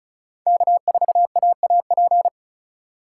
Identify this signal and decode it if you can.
Morse code, heard as a keyed tone, about 36 wpm, 700 Hz. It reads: K4AAP
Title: USA Call Signs - 5 Characters - 4x Character Spacing - 35wpm